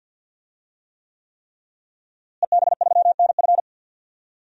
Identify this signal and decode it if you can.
Morse code, heard as a keyed tone, about 50 wpm, 700 Hz. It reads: EB3NF
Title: DXCC-100 Call Signs - 5 Characters - Repeated 3x - 50wpm